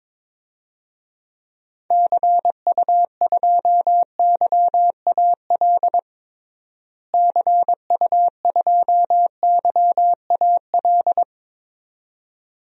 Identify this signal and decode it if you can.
Morse code, heard as a keyed tone, about 22 wpm, 700 Hz. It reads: CU2YAL CU2YAL
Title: DXCC-100 Call Signs - 6 Characters - Repeated 2x - 22wpm